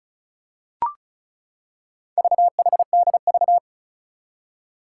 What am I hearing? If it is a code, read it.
Morse code, heard as a keyed tone, about 36 wpm, 700 Hz. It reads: VHDV
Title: Four Challenging Letter-Number Combinations 35wpm